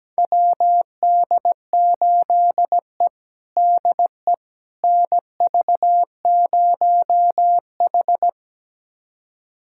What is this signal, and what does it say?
Morse code, heard as a keyed tone, about 17 wpm, 700 Hz. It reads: WD8E DE NV0H
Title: USA Callsign DE Callsign - 4 Characters - 17wpm